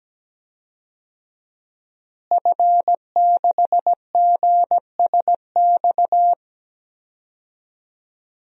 Morse code, heard as a keyed tone, about 17 wpm, 700 Hz. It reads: F6GSX
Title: DXCC-100 Call Signs - 5 Characters - Repeated 2x - 17wpm